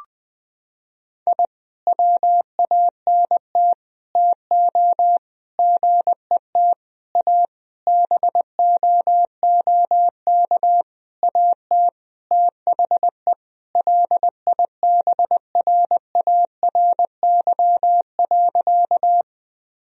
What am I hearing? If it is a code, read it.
Morse code, heard as a keyed tone, about 20 wpm, 700 Hz. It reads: I WANT TO GET A BOOK AT THE LIBRARY.
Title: Sentences from Top 1000 Words - Review Part 1 - 20wpm